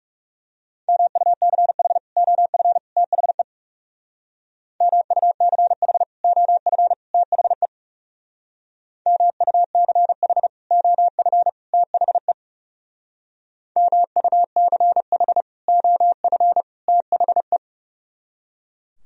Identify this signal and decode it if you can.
Morse code, heard as a keyed tone, about 36 wpm, 700 Hz. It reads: MUCH OF THE MUCH OF THE MUCH OF THE MUCH OF THE
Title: Speed-Racing - Sets of 3 Words 30wpm